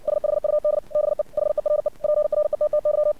Morse code, heard as a keyed tone, about 60 wpm, 600 Hz. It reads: SHUD BE HERE PRETTY
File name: archived / CRSnet